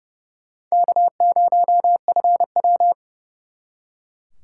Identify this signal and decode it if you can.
Morse code, heard as a keyed tone, about 39 wpm, 700 Hz. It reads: K0FW